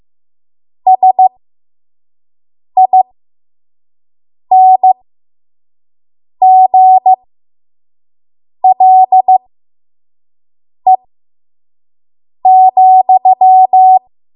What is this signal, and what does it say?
Morse code, heard as a keyed tone, about 15 wpm, spaced as for 5 wpm, 750 Hz. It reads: SINGLE,
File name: Morse / 05WPM